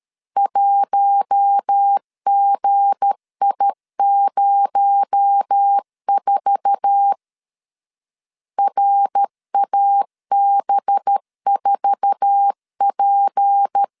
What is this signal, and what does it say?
Morse code, heard as a keyed tone, about 13 wpm, 800 Hz. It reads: 1GI04 RAB4P
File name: LEZIONI / 9 2 12wpm 2301